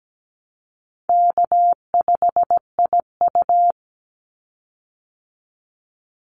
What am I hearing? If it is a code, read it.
Morse code, heard as a keyed tone, about 17 wpm, 700 Hz. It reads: K5IU
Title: Easy Call Signs Repeated 3x - 17wpm